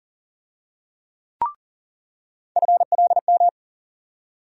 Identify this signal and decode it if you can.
Morse code, heard as a keyed tone, about 40 wpm, 700 Hz. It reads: FLM
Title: USA Call Sign Postfixes - 3 Letters 40wpm